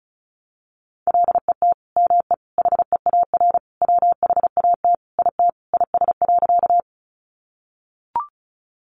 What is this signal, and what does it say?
Morse code, heard as a keyed tone, about 34 wpm, 700 Hz. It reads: LET ME HEAR WHAT IT IS.
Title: Sentences from Top 300 Words 35wpm